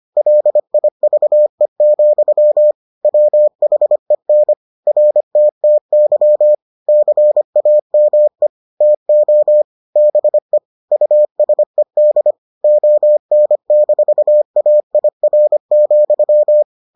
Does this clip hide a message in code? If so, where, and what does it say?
Yes: Morse code from 0.2 to 16.6 s, heard as a keyed tone, about 25 wpm, 600 Hz. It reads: LIVE, WHEN RTTY CAME TO BE USED ON-AIR,